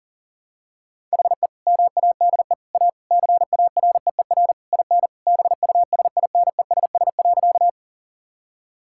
Morse code, heard as a keyed tone, about 40 wpm, 700 Hz. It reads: HE MADE A CAREER IN BUSINESS.
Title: Sentences from Top 1000 Words - Review Part 6 - 40wpm